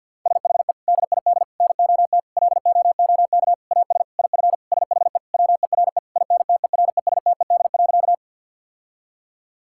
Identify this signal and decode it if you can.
Morse code, heard as a keyed tone, about 50 wpm, 700 Hz. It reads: SHE DID NOT LOOK AS IF SHE WERE INTERESTED.
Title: Sentences from Top 1000 Words - Review Part 6 - 50wpm